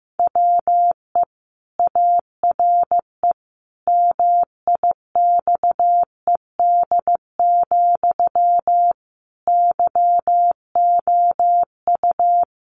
Morse code, heard as a keyed tone, about 15 wpm, 700 Hz. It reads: WE ARE MIXED, YOU
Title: Sentences from Top 1500 Words 15wpm